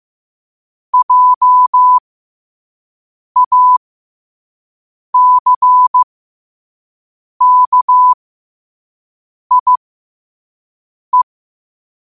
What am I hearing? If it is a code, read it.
Morse code, heard as a keyed tone, about 15 wpm, spaced as for 5 wpm, 1000 Hz. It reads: JACKIE
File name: qotdfiles / 5wpm